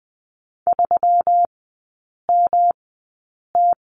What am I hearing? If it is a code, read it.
Morse code, heard as a keyed tone, about 20 wpm, spaced as for 8 wpm, 700 Hz. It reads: 3MT
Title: 191 Lesson 31e - Letter v - Three Characters 20wpm